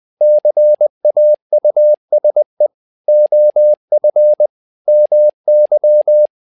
Morse code, heard as a keyed tone, about 20 wpm, 600 Hz. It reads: CAUSE OF MY
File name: learning center / tower trouble / tower trouble 20wpm